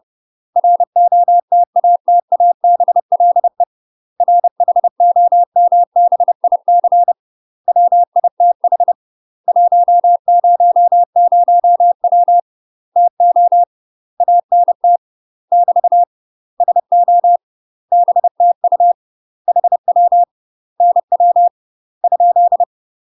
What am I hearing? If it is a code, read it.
Morse code, heard as a keyed tone, about 30 wpm, 700 Hz. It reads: ROTATABLE RHOMBIC WITH 100W TO ANT <BT> SO BTU HW NW ?
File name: UK MASTER QSOs TRAINING / 30WPM / UK MASTER QSOs 0021